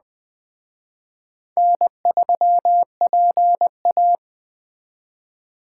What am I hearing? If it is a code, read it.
Morse code, heard as a keyed tone, about 20 wpm, 700 Hz. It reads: N3PA